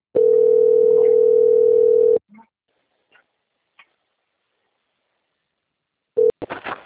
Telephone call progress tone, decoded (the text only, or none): ringback tone